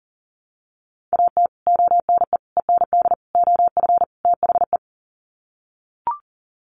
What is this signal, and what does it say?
Morse code, heard as a keyed tone, about 40 wpm, 700 Hz. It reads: AT ONE END OF THE
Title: Sets of 5 Words Encore 40wpm